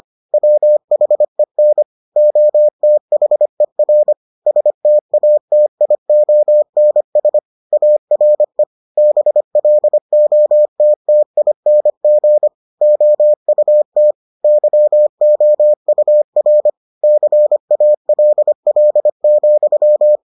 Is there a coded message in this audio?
Yes: Morse code from 0.3 to 20.2 s, heard as a keyed tone, about 25 wpm, 600 Hz. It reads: WHEN OTHER STATIONS ARE BLOTTING OUT YOUR CALL,